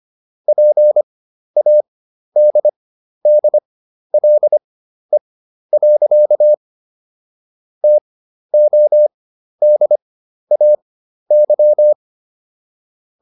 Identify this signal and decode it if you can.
Morse code, heard as a keyed tone, about 25 wpm, spaced as for 12 wpm, 600 Hz. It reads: PADDLE. TODAY